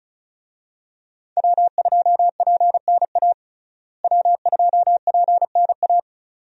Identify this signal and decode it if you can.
Morse code, heard as a keyed tone, about 36 wpm, 700 Hz. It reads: W2PNA W2PNA
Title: USA Call Signs - 5 Characters - Repeated 2x - 35wpm